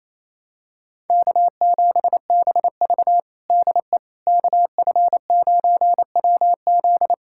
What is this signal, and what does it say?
Morse code, heard as a keyed tone, about 28 wpm, 700 Hz. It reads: K7BV DE KF9WZ
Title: K1USN SST QSO Practice 28wpm